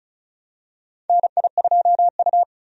Morse code, heard as a keyed tone, about 36 wpm, 700 Hz. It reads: NI2U
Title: USA Call Signs - 4 Characters - 35wpm